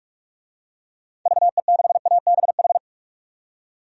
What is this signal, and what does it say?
Morse code, heard as a keyed tone, about 46 wpm, 700 Hz. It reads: VE6ABH
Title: DXCC-100 Call Signs - 6 Characters - 45wpm